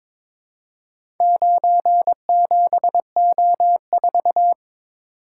Morse code, heard as a keyed tone, about 22 wpm, 700 Hz. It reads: 97O4
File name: practice / 22wpm / Four Letter-Number Combinations - 22wpm